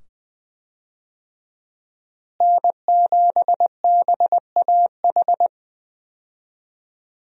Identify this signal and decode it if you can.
Morse code, heard as a keyed tone, about 21 wpm, 700 Hz. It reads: N7BAH
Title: USA Call Signs - 5 Characters - Repeated 2x - 20wpm